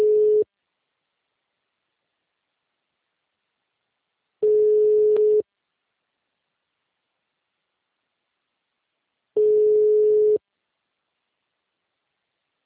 Ringback tone (Europe, 425 Hz).
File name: out- -2030- - - .76558